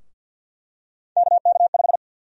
Morse code, heard as a keyed tone, about 52 wpm, 700 Hz. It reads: KK5
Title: USA Call Sign Prefixes - Repeated 2x - 50wpm